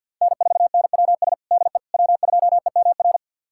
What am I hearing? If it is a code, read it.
Morse code, heard as a keyed tone, about 49 wpm, 700 Hz. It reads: N4NWS DE W2EMR
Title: USA Callsign DE Callsign - 5 Characters - 50wpm